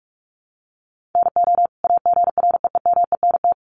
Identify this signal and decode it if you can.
Morse code, heard as a keyed tone, about 44 wpm, 700 Hz. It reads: NO AGREEMENT
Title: Sentences from Top 1500 Words 45wpm